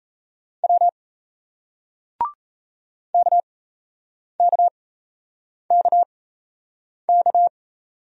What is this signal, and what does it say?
Morse code, heard as a keyed tone, about 38 wpm, spaced as for 8 wpm, 700 Hz. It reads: W KKKK